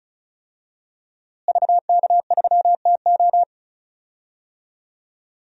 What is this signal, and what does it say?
Morse code, heard as a keyed tone, about 36 wpm, 700 Hz. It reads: VK3TO
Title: DXCC-100 Call Signs - 5 Characters - Repeated 3x - 35wpm